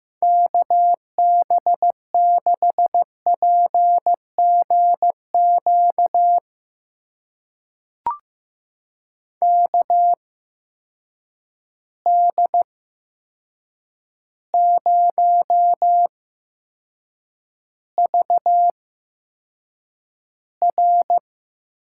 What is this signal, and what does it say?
Morse code, heard as a keyed tone, about 15 wpm, spaced as for 7 wpm, 700 Hz. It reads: KB6PGQ KD0VR
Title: USA Call Signs - 6 Characters - 8x Character Spacing - 15wpm